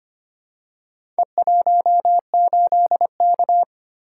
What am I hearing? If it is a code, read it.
Morse code, heard as a keyed tone, about 25 wpm, 700 Hz. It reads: E18K
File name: practice / 25wpm / Four Letter-Number Combinations - 25wpm